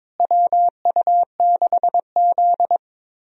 Morse code, heard as a keyed tone, about 22 wpm, 700 Hz. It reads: WU6Z
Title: USA Call Signs - 4 Characters - Repeated 2x - 22wpm